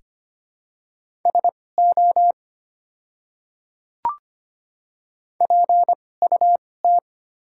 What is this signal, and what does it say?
Morse code, heard as a keyed tone, about 25 wpm, spaced as for 18 wpm, 700 Hz. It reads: SO PUT
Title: 2x Character Spacing - Top 200 Words - 25wpm